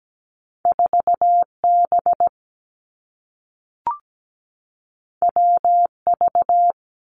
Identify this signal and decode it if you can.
Morse code, heard as a keyed tone, about 17 wpm, 700 Hz. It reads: 4 B W V
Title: Two Challenging Letter-Number Combinations 17wpm